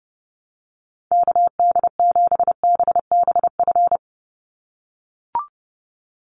Morse code, heard as a keyed tone, about 30 wpm, 700 Hz. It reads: KD7BBF
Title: USA Call Signs 30wpm